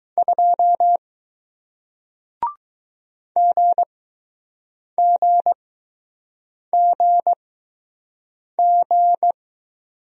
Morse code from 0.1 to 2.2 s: 2

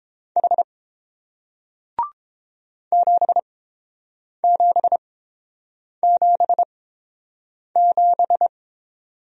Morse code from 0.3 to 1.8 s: H

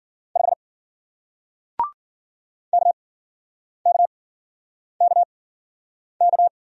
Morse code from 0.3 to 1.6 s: 5

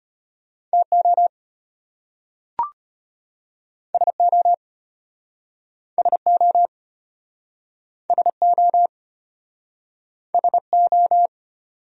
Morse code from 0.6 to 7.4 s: TO SO SO

